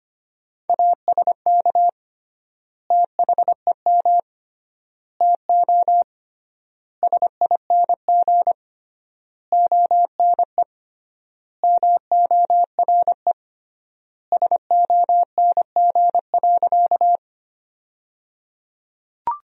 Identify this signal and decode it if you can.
Morse code, heard as a keyed tone, about 25 wpm, 700 Hz. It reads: ASK THEM TO SING ONE MORE SONG.
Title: 3x Word-Spacing - Sentences from Top 400 Words - 25wpm